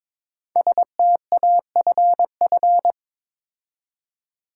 Morse code, heard as a keyed tone, about 22 wpm, 700 Hz. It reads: STAFF
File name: 5 Letter Words - First 3 Letters Given - 22wpm